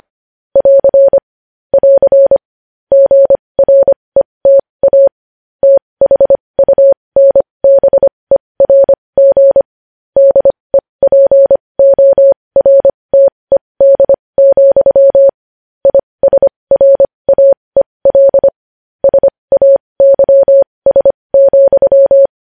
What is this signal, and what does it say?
Morse code, heard as a keyed tone, about 25 wpm, spaced as for 20 wpm, 550 Hz. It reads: <AR> <AR> GRETA THUNBERG DEPORTED, ISRAEL SAYS,
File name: podcast / News headlines-1x-25x20